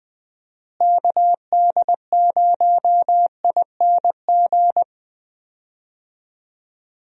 Morse code, heard as a keyed tone, about 20 wpm, 700 Hz. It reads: KD0ING
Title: USA Call Signs - 6 Characters - Repeated 2x - 20wpm